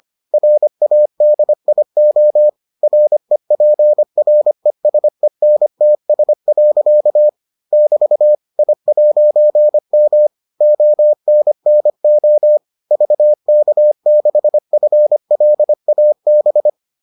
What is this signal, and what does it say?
Morse code, heard as a keyed tone, about 25 wpm, 600 Hz. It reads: RADIO REPRESENTS. <BT> I'M ONNO VK6FLAB